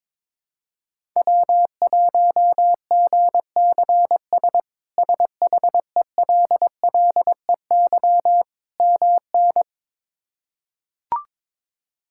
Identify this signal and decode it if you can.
Morse code, heard as a keyed tone, about 22 wpm, 700 Hz. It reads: W1GCS SHELLEY MN
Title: K1USN SST QSO Practice - 22wpm